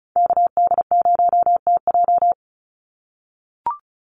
Morse code, heard as a keyed tone, about 34 wpm, 700 Hz. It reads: KD0TJ
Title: USA Call Signs 35wpm